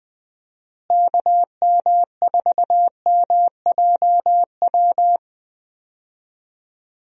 Morse code, heard as a keyed tone, about 21 wpm, 700 Hz. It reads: KM4MJW
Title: USA Call Signs - 6 Characters - Repeated 3x - 20wpm